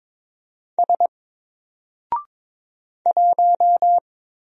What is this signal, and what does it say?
Morse code, heard as a keyed tone, about 22 wpm, 700 Hz. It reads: S 1
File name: Single-Letter-Number-22wpm